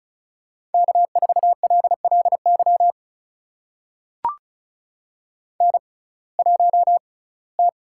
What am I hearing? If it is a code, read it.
Morse code, heard as a keyed tone, about 36 wpm, spaced as for 22 wpm, 700 Hz. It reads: K4LLY N1T
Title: USA Call Signs - 5 Characters - 6x Character Spacing - 35wpm